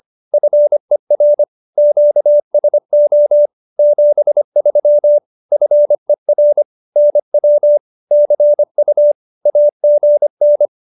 Morse code, heard as a keyed tone, about 25 wpm, 590 Hz. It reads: FER QSO 73 FER NW CU AGN
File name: qso201b 25